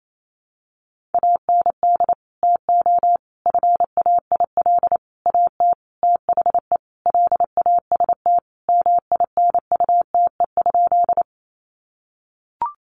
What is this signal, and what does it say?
Morse code, heard as a keyed tone, about 28 wpm, 700 Hz. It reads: AND TO FAIL AT THE LAST MINUTE?
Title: Sentences from Top 1100 Words 28wpm